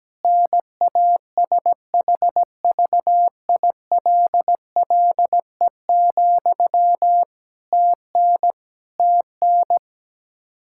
Morse code, heard as a keyed tone, about 17 wpm, 700 Hz. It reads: NASHVILLE, TN TN
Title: Practice QSO - Exchange 1 - Beginning - 17wpm